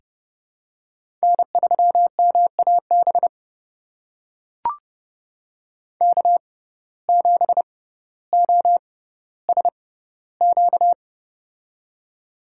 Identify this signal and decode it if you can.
Morse code, heard as a keyed tone, about 30 wpm, spaced as for 15 wpm, 700 Hz. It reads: N3MAB K7OSQ
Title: USA Call Signs - 5 Characters - 6x Character Spacing - 30wpm